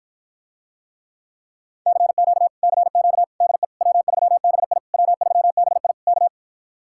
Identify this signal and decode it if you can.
Morse code, heard as a keyed tone, about 57 wpm, 700 Hz. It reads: CQ CQ DE W3BI W3BI K